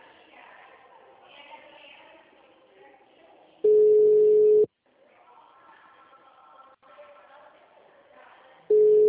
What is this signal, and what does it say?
Ringback tone (Europe, 425 Hz), with speech.
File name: out- -2014- - - .16440